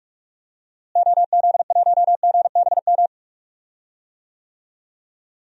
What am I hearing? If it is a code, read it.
Morse code, heard as a keyed tone, about 46 wpm, 700 Hz. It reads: OZ1GBM
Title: DXCC-100 Call Signs - 6 Characters - Repeated 3x - 45wpm